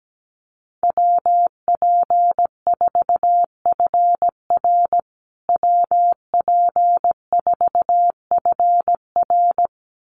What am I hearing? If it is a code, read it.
Morse code, heard as a keyed tone, about 17 wpm, 700 Hz. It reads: WP4FR WP4FR
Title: USA Call Signs Repeated 2x 17wpm